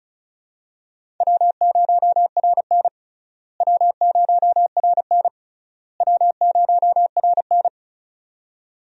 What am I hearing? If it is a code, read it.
Morse code, heard as a keyed tone, about 36 wpm, 700 Hz. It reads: W0RN W0RN W0RN
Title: USA Call Signs - 4 Characters - Repeated 3x - 35wpm